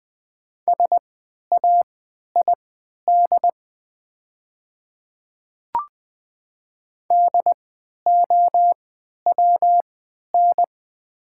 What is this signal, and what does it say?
Morse code, heard as a keyed tone, about 20 wpm, spaced as for 11 wpm, 700 Hz. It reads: SAID DOWN
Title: 3x Character Spacing - Top 100 Words - 20wpm